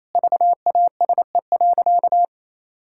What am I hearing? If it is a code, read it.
Morse code, heard as a keyed tone, about 28 wpm, 700 Hz. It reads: VASE.
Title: Pangram Challenge v2 - 28wpm